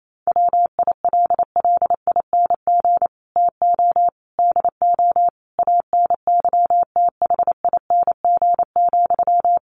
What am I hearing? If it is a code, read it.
Morse code, heard as a keyed tone, about 28 wpm, 700 Hz. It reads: WILLING TO DO ANYTHING,
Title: Sentences from Top 1400 Words 28wpm